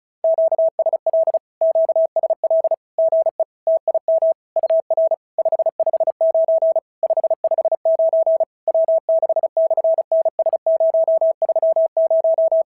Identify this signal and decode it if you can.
Morse code, heard as a keyed tone, about 34 wpm, 650 Hz. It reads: QSL QSL GE TIM UR 559 559 W6/NS030
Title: SOTA QSO Practice 35wpm